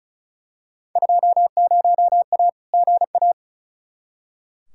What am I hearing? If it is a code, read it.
Morse code, heard as a keyed tone, about 34 wpm, 700 Hz. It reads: 20A GA
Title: Field Day Exchanges Encore - 35wpm